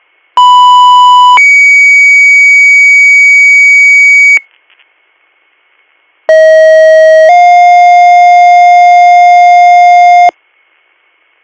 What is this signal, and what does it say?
Two-tone page, heard as a one-second tone, then a three-second tone, about 990 then 2250 Hz, about 640 then 710 Hz. It is heard from 0.4 to 10.3 s.